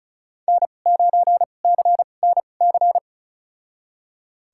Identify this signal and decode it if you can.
Morse code, heard as a keyed tone, about 34 wpm, spaced as for 25 wpm, 700 Hz. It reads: N9CNC
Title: USA Call Signs - 5 Characters - 2x Character Spacing - 35wpm